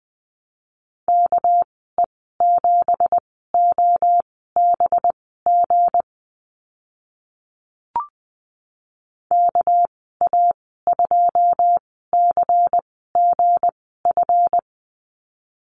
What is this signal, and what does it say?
Morse code, heard as a keyed tone, about 20 wpm, spaced as for 14 wpm, 700 Hz. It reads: KE7OBG KA2CGF
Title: 231 Lesson 37g - Letter x - Call Signs 20wpm